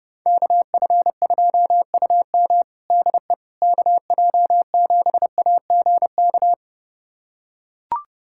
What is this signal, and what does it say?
Morse code, heard as a keyed tone, about 30 wpm, 700 Hz. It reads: KF2UM DE KJ7AGK
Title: USA Callsign DE Callsign 30wpm